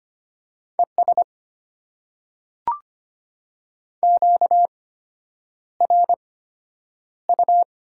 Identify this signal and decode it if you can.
Morse code, heard as a keyed tone, about 25 wpm, spaced as for 9 wpm, 700 Hz. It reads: ES QRU
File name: Top 90 QSO Elements - 8x Character Spacing - 25wpm